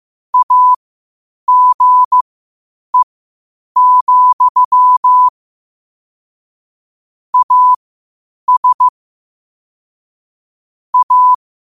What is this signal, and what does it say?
Morse code, heard as a keyed tone, about 15 wpm, spaced as for 8 wpm, 1000 Hz. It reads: AGE, AS A
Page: qotd-at-8-wpm